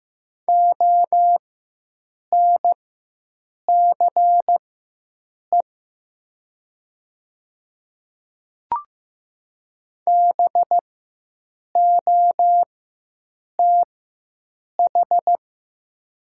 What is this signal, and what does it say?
Morse code, heard as a keyed tone, about 15 wpm, spaced as for 7 wpm, 700 Hz. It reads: ONCE BOTH
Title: 4x Character Spacing - Top 300 Words - 15wpm